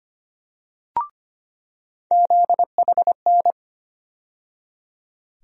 Morse code, heard as a keyed tone, about 25 wpm, 700 Hz. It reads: ZHN
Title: USA Call Sign Postfixes - 3 Letters 25wpm